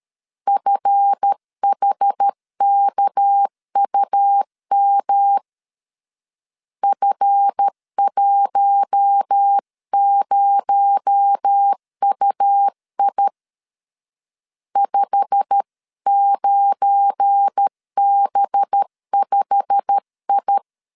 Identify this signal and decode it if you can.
Morse code, heard as a keyed tone, about 13 wpm, 800 Hz. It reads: FHKUM F10UI 59B5I
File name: LEZIONI / 8 YJ45 / 8 YJ45 12wpm 2151